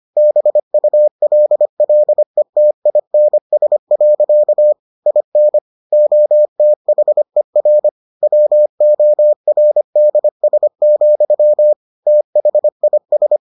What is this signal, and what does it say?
Morse code, heard as a keyed tone, about 25 wpm, 600 Hz. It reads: BULLETINS. IN OTHER WORDS, THIS